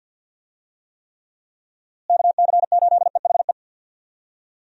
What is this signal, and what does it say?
Morse code, heard as a keyed tone, about 52 wpm, 700 Hz. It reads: KC8EHE
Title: USA Call Signs - 6 Characters - Repeated 3x - 50wpm